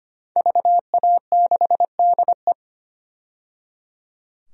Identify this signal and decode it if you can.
Morse code, heard as a keyed tone, about 25 wpm, 700 Hz. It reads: VA6DE
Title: DXCC-100 Call Signs - 5 Characters - 25wpm